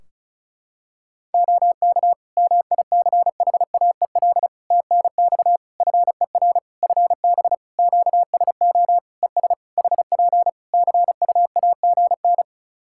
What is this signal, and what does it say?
Morse code, heard as a keyed tone, about 34 wpm, 700 Hz. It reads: OK MICHAEL TNX FER FB QSO ES HP CUAGN
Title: K1USN SST QSO Practice 35wpm